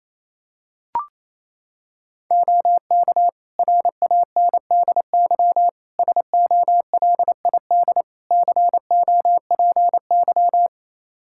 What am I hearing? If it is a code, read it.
Morse code, heard as a keyed tone, about 28 wpm, 700 Hz. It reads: OK RANDY SOLID COPY
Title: Practice QSO - Exchange 3 - More Info - 28wpm